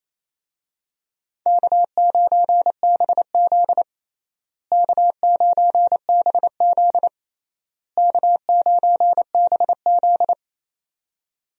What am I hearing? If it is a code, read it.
Morse code, heard as a keyed tone, about 28 wpm, 700 Hz. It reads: K9BZ K9BZ K9BZ
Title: USA Call Signs - 4 Characters - Repeated 3x - 28wpm